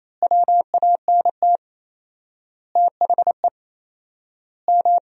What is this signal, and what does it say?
Morse code, heard as a keyed tone, about 28 wpm, 700 Hz. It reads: WANT THE M
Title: 4x Word-Spacing - Sentences from Top 400 Words - 28wpm